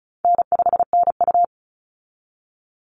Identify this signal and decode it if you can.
Morse code, heard as a keyed tone, about 34 wpm, 700 Hz. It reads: N5NU
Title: Easy Call Signs Repeated 3x - 35wpm